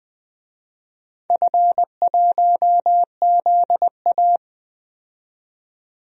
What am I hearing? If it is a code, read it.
Morse code, heard as a keyed tone, about 20 wpm, 700 Hz. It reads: F1ZA
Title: DXCC-100 Call Signs - 4 Characters - 20wpm